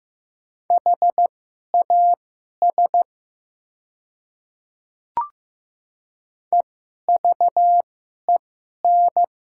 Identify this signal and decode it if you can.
Morse code, heard as a keyed tone, about 15 wpm, spaced as for 11 wpm, 700 Hz. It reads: HAS EVEN
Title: 2x Character Spacing - Top 200 Words - 15wpm